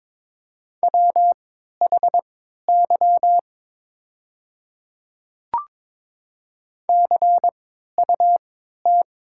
Morse code, heard as a keyed tone, about 22 wpm, spaced as for 12 wpm, 700 Hz. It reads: WHY CUT
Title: 3x Character Spacing - Top 300 Words - 22wpm